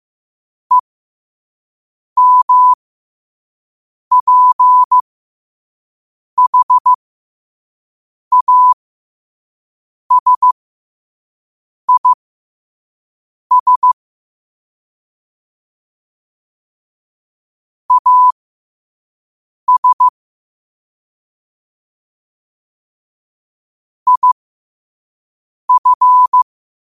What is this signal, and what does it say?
Morse code, heard as a keyed tone, about 15 wpm, spaced as for 5 wpm, 1000 Hz. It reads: EMPHASIS AS IF
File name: qotdfiles / 5wpm